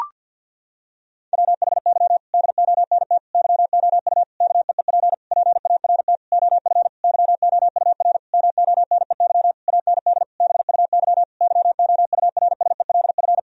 Morse code, heard as a keyed tone, about 50 wpm, 700 Hz. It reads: WHY DONT YOU KEEP PART OF YOUR MONEY AND BUY YOURSELF
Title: Sentences from Top 1000 Words - Review Part 4 - 50wpm